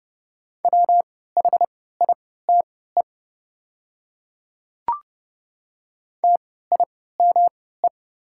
Morse code, heard as a keyed tone, about 30 wpm, spaced as for 17 wpm, 700 Hz. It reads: WHITE TIME
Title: 3x Character Spacing - Top 300 Words - 30wpm